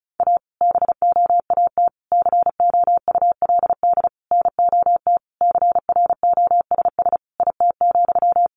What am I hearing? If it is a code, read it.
Morse code, heard as a keyed tone, about 34 wpm, 700 Hz. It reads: A BOAT COULD NOT CROSS IT,
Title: Sentences from Top 1400 Words 35wpm